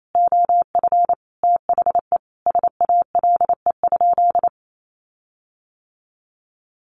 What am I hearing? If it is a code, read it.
Morse code, heard as a keyed tone, about 28 wpm, 700 Hz. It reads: OF THE SALE?
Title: Sentences from Top 1400 Words 28wpm